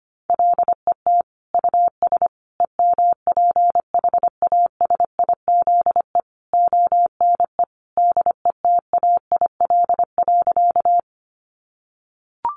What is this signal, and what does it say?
Morse code, heard as a keyed tone, about 25 wpm, 700 Hz. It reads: LET US EMPHASIZE ONE DETAIL.
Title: Sentences from Top 1500 Words 25wpm